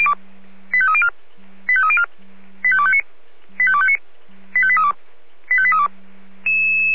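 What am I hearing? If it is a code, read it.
Five-tone selective call (ZVEI), heard as quick runs of five tones, tones of about 70 ms. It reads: 85304 85304 85259 85259 85832 85832
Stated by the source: Received on an SDR.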